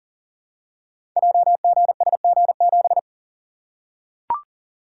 Morse code, heard as a keyed tone, about 38 wpm, 700 Hz. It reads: JGSG7
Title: Code Groups - 5 Letter:Numbers - 40wpm